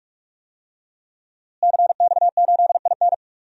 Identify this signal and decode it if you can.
Morse code, heard as a keyed tone, about 43 wpm, 700 Hz. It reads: CX8IN
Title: DXCC-100 Call Signs - 5 Characters - Repeated 2x - 45wpm